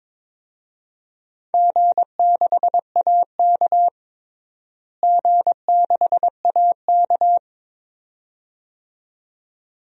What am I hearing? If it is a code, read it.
Morse code, heard as a keyed tone, about 22 wpm, 700 Hz. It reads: G6AK G6AK
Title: DXCC-100 Call Signs - 4 Characters - Repeated 2x - 22wpm